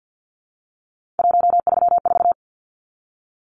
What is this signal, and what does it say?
Morse code, heard as a keyed tone, about 50 wpm, 700 Hz. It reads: 134
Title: Numbers - 3-Digit 50wpm